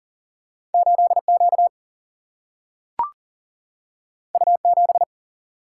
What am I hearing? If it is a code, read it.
Morse code, heard as a keyed tone, about 40 wpm, 700 Hz. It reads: 8 Q U 7
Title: Single-Letter Single-Number 40wpm